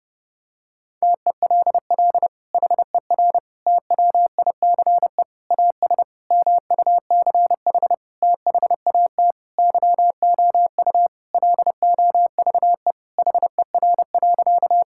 Morse code, heard as a keyed tone, about 30 wpm, 700 Hz. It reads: TELL HER TWICE AS MUCH THAT YOU LOVE HER.
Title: Sentences from Top 1600 Words 30wpm